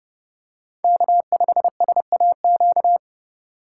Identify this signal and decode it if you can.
Morse code, heard as a keyed tone, about 30 wpm, 700 Hz. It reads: K5SAQ